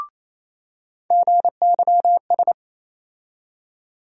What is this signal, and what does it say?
Morse code, heard as a keyed tone, about 28 wpm, 700 Hz. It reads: GYS
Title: USA Call Sign Postfixes - 3 Letters 28wpm